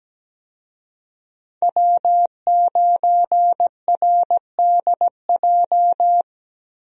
Morse code, heard as a keyed tone, about 17 wpm, 700 Hz. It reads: W9RDJ
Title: USA Call Signs - 5 Characters - Repeated 2x - 17wpm